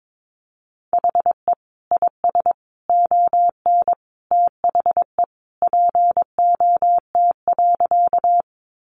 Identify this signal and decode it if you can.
Morse code, heard as a keyed tone, about 22 wpm, 700 Hz. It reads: HE IS ON THE POT.
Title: Sentences from Top 1000 Words 22wpm